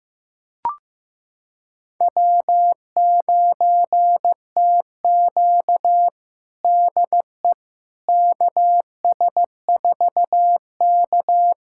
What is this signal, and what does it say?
Morse code, heard as a keyed tone, about 15 wpm, 700 Hz. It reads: W9TQ DE KS4K
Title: USA Callsign DE Callsign - 4 Characters - 15wpm